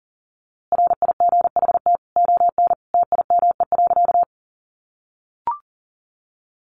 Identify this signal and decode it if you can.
Morse code, heard as a keyed tone, about 40 wpm, 700 Hz. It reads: RIGHT ON TIME.
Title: A Sound Of Thunder 40wpm